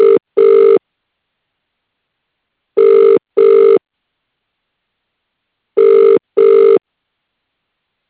Ringback tone (UK), 400 + 450 Hz.